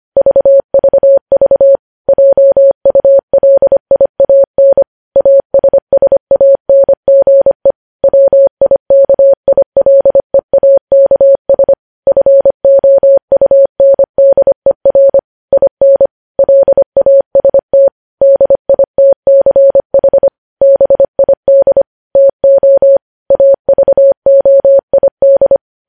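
Morse code, heard as a keyed tone, about 25 wpm, 550 Hz. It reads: VVV JULIAN ASSANGE WIKILEAKS FOUNDER IN LAST DITCH BID TO AVOID
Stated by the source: Morse news headlines at 25 WPM.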